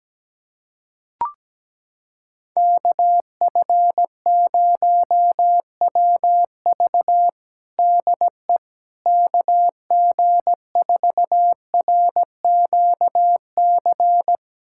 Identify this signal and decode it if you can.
Morse code, heard as a keyed tone, about 17 wpm, 700 Hz. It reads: KF0WV DE KG4RQC
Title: USA Callsign DE Callsign 17wpm